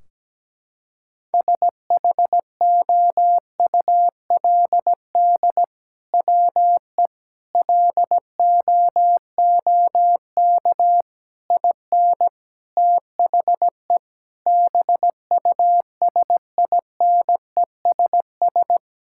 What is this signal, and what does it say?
Morse code, heard as a keyed tone, about 17 wpm, 700 Hz. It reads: SHOULD WE LOOK IN THE BUSINESS
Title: Sentences from Top 1600 Words 17wpm